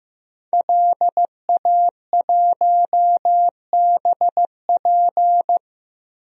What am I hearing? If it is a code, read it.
Morse code, heard as a keyed tone, about 15 wpm, 700 Hz. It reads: LA1BP
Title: DXCC-100 Call Signs - 5 Characters - Repeated 3x - 15wpm